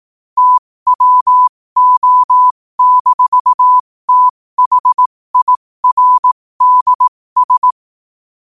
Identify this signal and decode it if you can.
Morse code, heard as a keyed tone, about 18 wpm, spaced as for 15 wpm, 1000 Hz. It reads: TWO-THIRDS
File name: qotdfiles / 15wpm